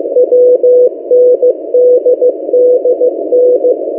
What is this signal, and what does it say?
Morse code, heard as a keyed tone, about 15 wpm, 500 Hz. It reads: WNDDN
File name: M08a- -1355utc-8096khz